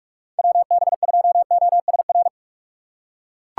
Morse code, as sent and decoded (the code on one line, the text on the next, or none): .-- -... ..--- --- ... .-.
WB2OSR